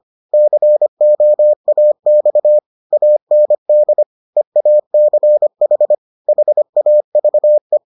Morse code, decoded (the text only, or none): COAX AND EACH HAVE